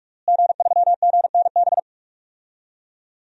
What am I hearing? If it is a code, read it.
Morse code, heard as a keyed tone, about 45 wpm, 700 Hz. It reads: G3GNB
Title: DXCC-100 Call Signs - 5 Characters - Repeated 3x - 45wpm